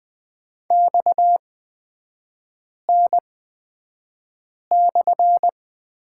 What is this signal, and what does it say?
Morse code, heard as a keyed tone, about 20 wpm, spaced as for 5 wpm, 700 Hz. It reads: XN/
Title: 229 Lesson 37e - Letter x - Three Characters 20wpm